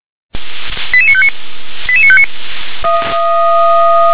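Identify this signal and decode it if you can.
Five-tone selective call (ZVEI), heard as quick runs of five tones, tones of about 70 ms. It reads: 88959 88959
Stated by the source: SDR-received ZVEI five-tone call, code 88841.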